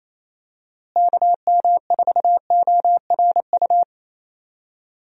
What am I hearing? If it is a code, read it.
Morse code, heard as a keyed tone, about 28 wpm, 700 Hz. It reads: KM4ORU
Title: USA Call Signs - 6 Characters - Repeated 2x - 28wpm